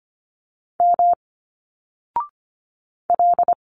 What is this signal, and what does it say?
Morse code, heard as a keyed tone, about 24 wpm, 700 Hz. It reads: M L
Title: Single Letters 25wpm